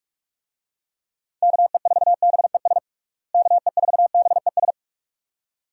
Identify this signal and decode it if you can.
Morse code, heard as a keyed tone, about 46 wpm, 700 Hz. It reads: KE4BES KE4BES
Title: USA Call Signs - 6 Characters - Repeated 2x - 45wpm